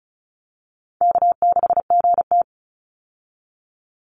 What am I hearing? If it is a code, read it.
Morse code, heard as a keyed tone, about 34 wpm, 700 Hz. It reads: K6GT
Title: Call Signs Repeated 3x - 35wpm